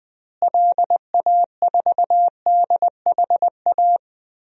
Morse code, heard as a keyed tone, about 21 wpm, 700 Hz. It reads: LA4DHA
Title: DXCC-100 Call Signs - 6 Characters - Repeated 3x - 20wpm